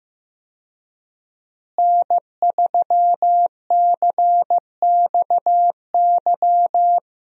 Morse code, heard as a keyed tone, about 16 wpm, 700 Hz. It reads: N3CXY
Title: USA Call Signs - 5 Characters - Repeated 3x - 15wpm